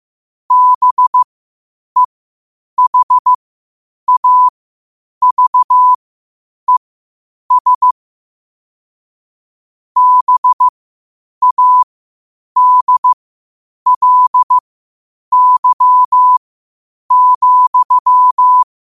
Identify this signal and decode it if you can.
Morse code, heard as a keyed tone, about 15 wpm, spaced as for 8 wpm, 1000 Hz. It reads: BEHAVES BADLY,
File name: qotdfiles / 8wpm